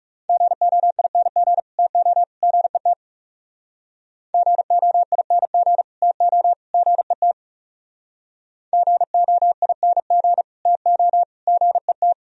Morse code, heard as a keyed tone, about 40 wpm, 700 Hz. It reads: GOING TO GET GOING TO GET GOING TO GET